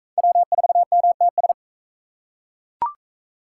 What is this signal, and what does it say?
Morse code, heard as a keyed tone, about 41 wpm, 700 Hz. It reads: W4MTS